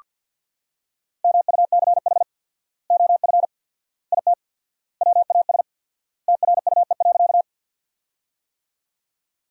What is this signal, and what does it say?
Morse code, heard as a keyed tone, about 49 wpm, 700 Hz. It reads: MUCH OF IT WAS TRUE.
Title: 4x Word-Spacing - Sentences from Top 400 Words - 50wpm